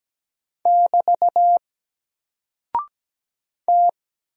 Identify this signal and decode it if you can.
Morse code, heard as a keyed tone, about 17 wpm, 700 Hz. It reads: <BT> T
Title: Single Character (Letter:Number:Punctuation:Prosign) 17wpm